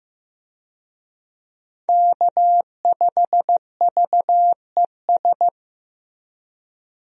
Morse code, heard as a keyed tone, about 15 wpm, 700 Hz. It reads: K5VES
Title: USA Call Signs - 5 Characters - Repeated 3x - 15wpm